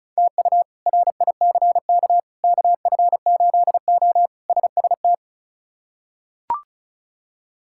Morse code, heard as a keyed tone, about 36 wpm, 700 Hz. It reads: TU RICK KF8O SST
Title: K1USN SST QSO Practice - 35wpm